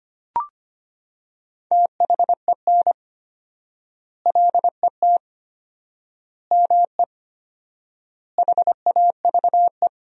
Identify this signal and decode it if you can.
Morse code, heard as a keyed tone, about 25 wpm, 700 Hz. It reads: THEN LET ME HAVE
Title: 4x Word-Spacing - Sentences from Top 300 Words - 25wpm